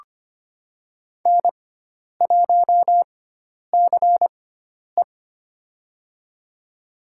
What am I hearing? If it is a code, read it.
Morse code, heard as a keyed tone, about 25 wpm, spaced as for 10 wpm, 700 Hz. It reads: N1CE
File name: Lesson 16g - Letter c - Call Signs 25wpm